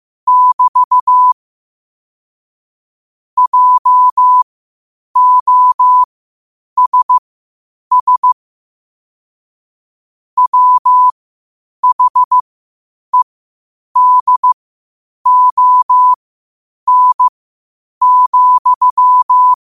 Morse code, heard as a keyed tone, about 15 wpm, spaced as for 8 wpm, 1000 Hz. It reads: <BT> JOSS WHEDON,
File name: qotdfiles / 8wpm